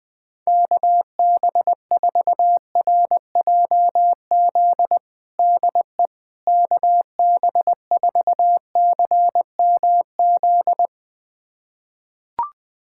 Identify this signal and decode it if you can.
Morse code, heard as a keyed tone, about 21 wpm, 700 Hz. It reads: KB4RJZ DE KB4CMZ
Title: USA Callsign DE Callsign - 6 Characters - 20wpm